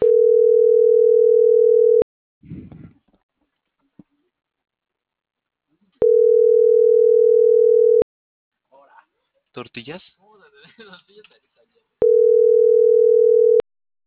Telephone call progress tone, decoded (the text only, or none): ringback tone